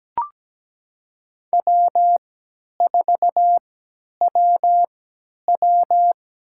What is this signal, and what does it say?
Morse code, heard as a keyed tone, about 17 wpm, spaced as for 10 wpm, 700 Hz. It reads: W4WW
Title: USA Call Signs - 4 Characters - 3x Character Spacing - 17wpm